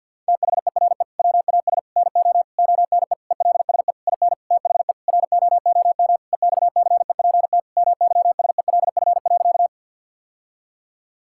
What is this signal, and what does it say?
Morse code, heard as a keyed tone, about 50 wpm, 700 Hz. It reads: THERE WAS NO ONE ELSE IN THE ROOM EXCEPT MYSELF.
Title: Sentences from Top 1600 Words 50wpm